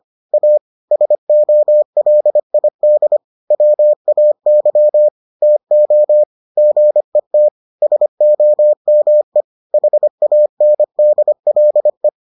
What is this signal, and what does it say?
Morse code, heard as a keyed tone, about 25 wpm, 600 Hz. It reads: A SOLID WAY TO GET SOME HANDLE